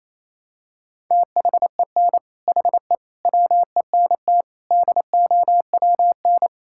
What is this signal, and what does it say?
Morse code, heard as a keyed tone, about 28 wpm, 700 Hz. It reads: THEN HE WENT DOWN
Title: Sentences from Top 1000 Words - Review Part 5 - 28wpm